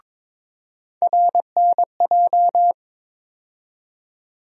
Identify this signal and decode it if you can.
Morse code, heard as a keyed tone, about 22 wpm, 700 Hz. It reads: RNJ
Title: USA Call Sign Postfixes - 3 Letters 22wpm